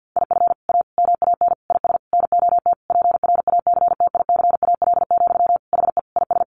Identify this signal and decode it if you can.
Morse code, heard as a keyed tone, about 50 wpm, 700 Hz. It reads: IF A MAN IS NOT PRACTICAL, HE IS
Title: Sentences from Top 1400 Words 50wpm